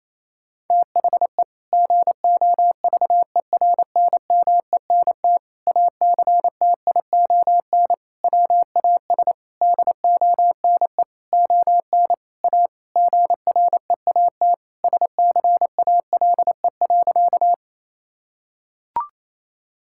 Morse code, heard as a keyed tone, about 28 wpm, 700 Hz. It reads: THE GOVERNMENT ACTION WAS DONE ON A GREAT SCALE.
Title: Sentences from Top 1000 Words - Review Part 5 - 28wpm